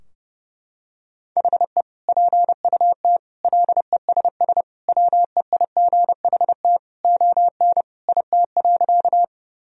Morse code, heard as a keyed tone, about 30 wpm, 700 Hz. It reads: HE PUT LESS WEIGHT ON IT.
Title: Sentences from Top 1000 Words - Review Part 3 - 30wpm